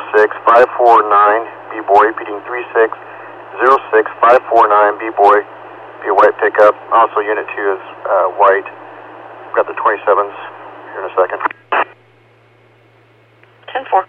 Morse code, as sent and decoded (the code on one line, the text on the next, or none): none